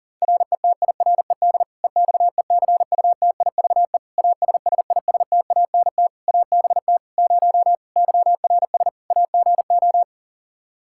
Morse code, heard as a keyed tone, about 40 wpm, 700 Hz. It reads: RETIRED EXECUTIVE ASSISTANT ABT 0 YRS AGO
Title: K1USN SST QSO Practice 40wpm